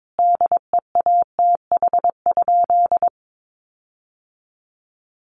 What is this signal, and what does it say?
Morse code, heard as a keyed tone, about 22 wpm, 700 Hz. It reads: DEATH?
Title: Sentences from Top 1500 Words 22wpm